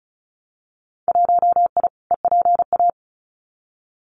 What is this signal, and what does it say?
Morse code, heard as a keyed tone, about 34 wpm, 700 Hz. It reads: 1I EPA
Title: Winter Field Day Exchanges Encore 35wpm